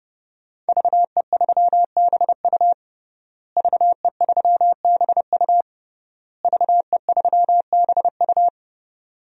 Morse code, heard as a keyed tone, about 30 wpm, 700 Hz. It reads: VE3BU VE3BU VE3BU
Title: DXCC-100 Call Signs - 5 Characters - Repeated 3x - 30wpm